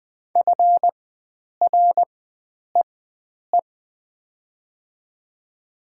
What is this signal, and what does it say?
Morse code, heard as a keyed tone, about 20 wpm, spaced as for 9 wpm, 700 Hz. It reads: FREE
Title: 4x Character Spacing - Top 500 Words - 20wpm